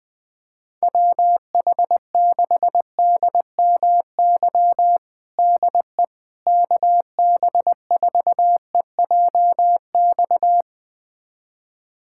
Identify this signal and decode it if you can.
Morse code, heard as a keyed tone, about 20 wpm, 700 Hz. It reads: WH6DMY DE KB4EJX
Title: USA Callsign DE Callsign - 6 Characters - 20wpm